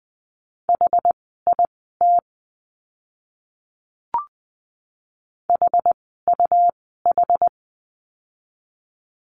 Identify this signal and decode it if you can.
Morse code, heard as a keyed tone, about 20 wpm, spaced as for 14 wpm, 700 Hz. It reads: HIT HUH
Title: 101 Lesson 17f - Letter u - Words 20wpm